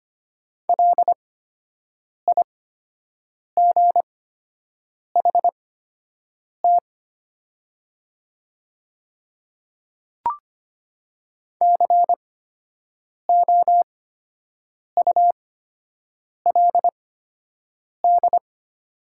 Morse code, heard as a keyed tone, about 25 wpm, spaced as for 7 wpm, 700 Hz. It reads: LIGHT COULD
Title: 8x Character Spacing - Top 300 Words - 25wpm